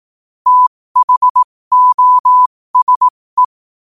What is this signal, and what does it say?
Morse code, heard as a keyed tone, about 18 wpm, spaced as for 15 wpm, 1000 Hz. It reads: THOSE
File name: qotdfiles / 15wpm